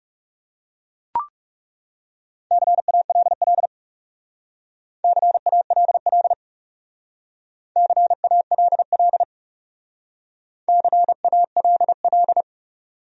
Morse code, heard as a keyed tone, about 38 wpm, 700 Hz. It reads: CALL CALL CALL CALL